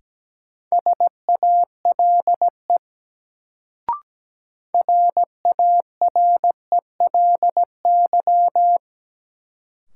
Morse code, heard as a keyed tone, about 17 wpm, 700 Hz. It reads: SALE RARELY
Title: Sentences from Top 1400 Words - New Word Drill Alt - 17wpm